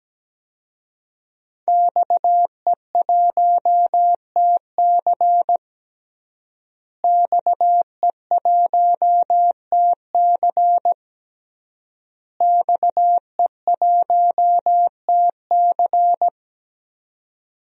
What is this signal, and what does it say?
Morse code, heard as a keyed tone, about 17 wpm, 700 Hz. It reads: XE1TC XE1TC XE1TC
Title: DXCC-100 Call Signs - 5 Characters - Repeated 3x - 17wpm